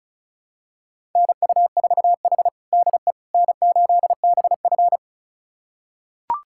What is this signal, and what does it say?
Morse code, heard as a keyed tone, about 36 wpm, 700 Hz. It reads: NU4H DE N8BF
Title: USA Callsign DE Callsign - 4 Characters - 35wpm